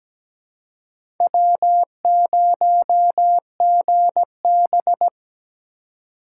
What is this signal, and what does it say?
Morse code, heard as a keyed tone, about 17 wpm, 700 Hz. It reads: W0GB
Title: USA Call Signs - 3x Character Spacing - 17wpm